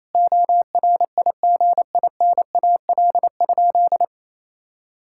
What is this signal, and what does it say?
Morse code, heard as a keyed tone, about 28 wpm, 700 Hz. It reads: ORIGINAL?
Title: Sentences from Top 1000 Words - Review Part 6 - 28wpm